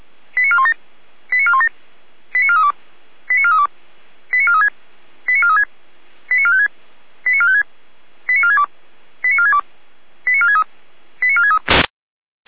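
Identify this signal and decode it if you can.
Five-tone selective call (ZVEI), heard as quick runs of five tones, tones of about 70 ms. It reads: 89417 89417 89432 89432 89436 89436 89456 89456 89462 89462 89463 89463
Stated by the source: ZVEI five-tone selective call 89380 received by SDR.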